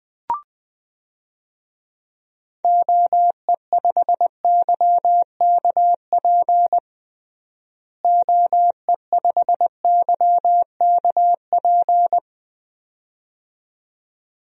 Morse code, heard as a keyed tone, about 20 wpm, 700 Hz. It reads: OE5YKP OE5YKP
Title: DXCC-100 Call Signs - 6 Characters - Repeated 2x - 20wpm